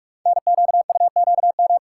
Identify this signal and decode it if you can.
Morse code, heard as a keyed tone, about 44 wpm, 700 Hz. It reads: NQUQM
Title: Code Groups - 5 Letters - 45wpm